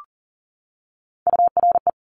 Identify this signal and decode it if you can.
Morse code, heard as a keyed tone, about 38 wpm, 700 Hz. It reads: URE
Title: Top Three Letter Combinations in Words 40wpm